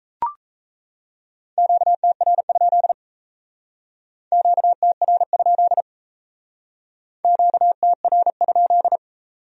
Morse code, heard as a keyed tone, about 37 wpm, 700 Hz. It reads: QTR? QTR? QTR?